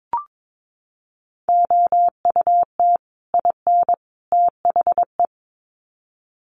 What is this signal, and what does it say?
Morse code, heard as a keyed tone, about 22 wpm, 700 Hz. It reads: OUT IN THE
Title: Sets of 3 Words 22wpm